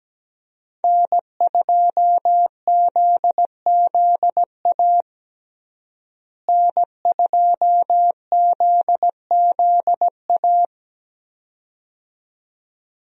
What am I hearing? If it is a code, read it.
Morse code, heard as a keyed tone, about 17 wpm, 700 Hz. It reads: N2ZZA N2ZZA
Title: USA Call Signs - 5 Characters - Repeated 2x - 17wpm